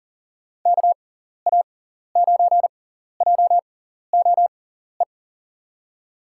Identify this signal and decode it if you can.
Morse code, heard as a keyed tone, about 42 wpm, spaced as for 14 wpm, 700 Hz. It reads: KA9JOE
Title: USA Call Signs - 6 Characters - 6x Character Spacing - 40wpm